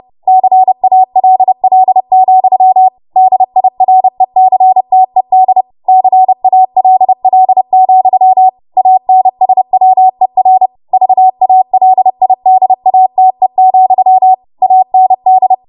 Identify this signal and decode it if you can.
Morse code, heard as a keyed tone, about 30 wpm, 750 Hz. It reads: CALL, DIRECTED CALL, ANSWER VALIDATE, AND
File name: Morse / 30WPM